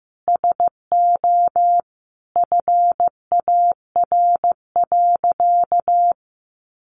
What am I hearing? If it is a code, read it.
Morse code, heard as a keyed tone, about 15 wpm, 700 Hz. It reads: SO FAR.
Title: Sentences from Top 800 Words 15wpm